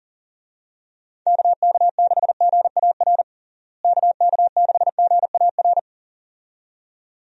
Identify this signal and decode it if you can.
Morse code, heard as a keyed tone, about 40 wpm, 700 Hz. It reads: KK6GAR KK6GAR
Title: USA Call Signs - 6 Characters - Repeated 2x - 40wpm